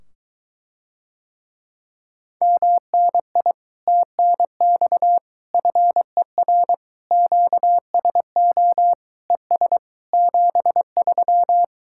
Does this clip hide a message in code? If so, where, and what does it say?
Yes: Morse code from 2.4 to 11.6 s, heard as a keyed tone, about 23 wpm, 700 Hz. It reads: MNI TNX FER QSO ES 73